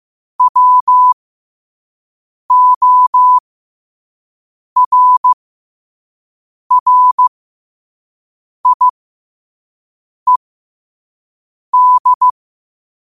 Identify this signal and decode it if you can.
Morse code, heard as a keyed tone, about 15 wpm, spaced as for 5 wpm, 1000 Hz. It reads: WORRIED